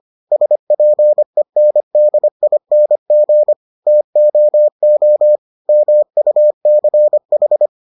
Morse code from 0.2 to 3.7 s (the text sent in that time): SPENDING